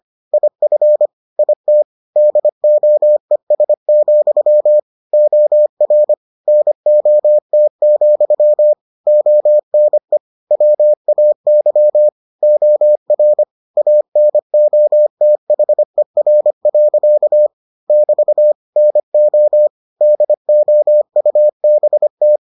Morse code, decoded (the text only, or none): IF IT DOES, OR NOT, ONE WAY OR ANOTHER. <BT> NO DOUBT